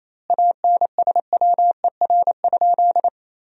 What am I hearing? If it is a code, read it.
Morse code, heard as a keyed tone, about 28 wpm, 700 Hz. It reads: ANSWER?
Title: 2x Word-Spacing - Sentences from Top 300 Words - 28wpm